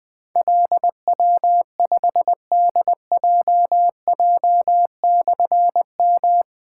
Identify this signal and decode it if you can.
Morse code, heard as a keyed tone, about 20 wpm, 700 Hz. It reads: LW5DJJ/M
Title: Difficult Call Signs 20wpm